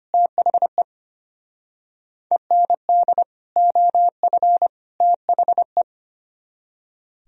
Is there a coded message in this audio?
Yes: Morse code from 0.1 to 5.8 s, heard as a keyed tone, about 32 wpm, 700 Hz. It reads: THE END OF THE